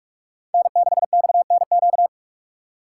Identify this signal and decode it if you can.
Morse code, heard as a keyed tone, about 45 wpm, 700 Hz. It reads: N6XNQ